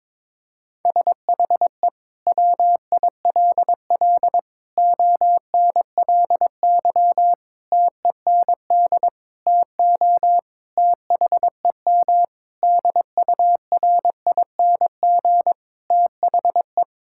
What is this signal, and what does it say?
Morse code, heard as a keyed tone, about 22 wpm, 700 Hz. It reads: SHE WILL ONLY TEND TO THEM DURING THE
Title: Sentences from Top 1000 Words - Review Part 5 - 22wpm